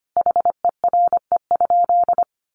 Morse code, heard as a keyed tone, about 25 wpm, 700 Hz. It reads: HERE?
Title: Sentences from Top 1400 Words 25wpm